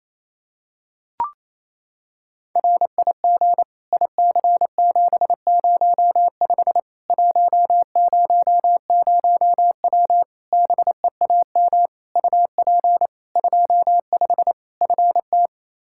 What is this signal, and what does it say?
Morse code, heard as a keyed tone, about 28 wpm, 700 Hz. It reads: RIG IC705 100W BEAM UP 25 FT
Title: K1USN SST QSO Practice 28wpm